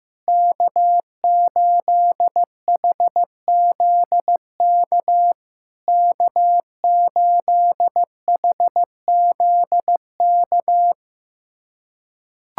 Morse code, heard as a keyed tone, about 15 wpm, 700 Hz. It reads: K8HZK K8HZK
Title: SOTA QSO Practice 15wpm